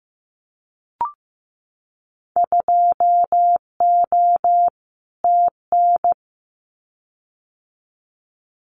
Morse code, heard as a keyed tone, about 15 wpm, 700 Hz. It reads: 2O TN
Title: Winter Field Day Exchanges 15wpm